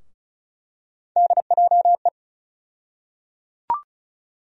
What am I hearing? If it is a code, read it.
Morse code, heard as a keyed tone, about 34 wpm, 700 Hz. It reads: DJE
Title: USA Call Sign Postfixes - 3 Letters 35wpm